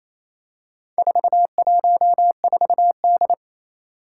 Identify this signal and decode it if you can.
Morse code, heard as a keyed tone, about 27 wpm, 700 Hz. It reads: 414D
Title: Four Challenging Letter-Number Combinations 28wpm